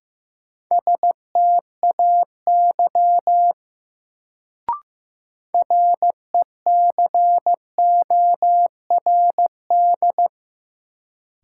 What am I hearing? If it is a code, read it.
Morse code, heard as a keyed tone, about 15 wpm, 700 Hz. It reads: STAY RECORD
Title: Sentences from Top 500 Words - New Word Drill - 15wpm